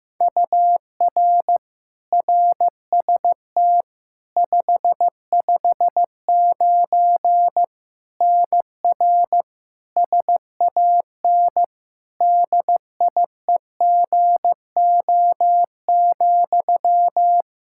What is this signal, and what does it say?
Morse code, heard as a keyed tone, about 15 wpm, 700 Hz. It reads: UR RST 559 NR SAN DIEGO,
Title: Practice QSO - Exchange 1 - Beginning - 15wpm